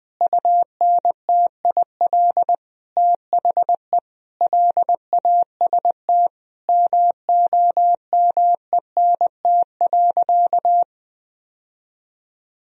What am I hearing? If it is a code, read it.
Morse code, heard as a keyed tone, about 20 wpm, 700 Hz. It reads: UNTIL THE LAST MOMENT.
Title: Sentences from Top 1000 Words - Review Part 1 - 20wpm